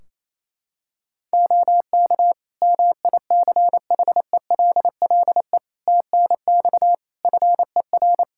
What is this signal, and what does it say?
Morse code, heard as a keyed tone, about 28 wpm, 700 Hz. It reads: OK MICHELLE TNX FER
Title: Practice QSO - Exchange 4 - Ending - 28wpm